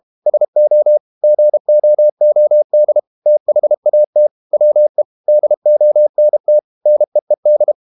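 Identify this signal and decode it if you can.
Morse code, heard as a keyed tone, about 32 wpm, 600 Hz. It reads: SO GOOD THAT WE DONT NEED
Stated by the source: Morse code practice at 32 wpm from CW med Gnister, reading Ken Follett's novel Never in English.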